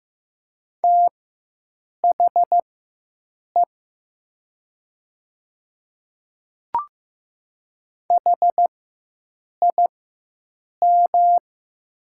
Morse code, heard as a keyed tone, about 15 wpm, spaced as for 7 wpm, 700 Hz. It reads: THE HIM
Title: 4x Character Spacing - Top 100 Words - 15wpm